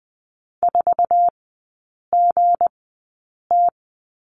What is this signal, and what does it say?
Morse code, heard as a keyed tone, about 20 wpm, spaced as for 8 wpm, 690 Hz. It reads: 4GT
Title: 172 Lesson 28e - Number 9 - Three Characters 20wpm